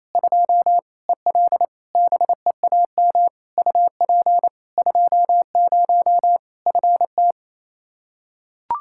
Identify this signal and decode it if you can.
Morse code, heard as a keyed tone, about 28 wpm, 700 Hz. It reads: 2 EL BEAM UP 20 FT
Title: Practice QSO - Exchange 2 - Info - 28wpm